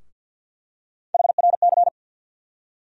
Morse code, heard as a keyed tone, about 48 wpm, 700 Hz. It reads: HRC
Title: USA Call Sign Postfixes - 3 Letters 50wpm